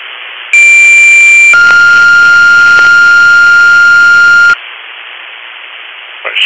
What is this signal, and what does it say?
Two-tone page, heard as a one-second tone, then a three-second tone, about 2350 then 1400 Hz, with speech. It is heard from 0.5 to 4.5 s.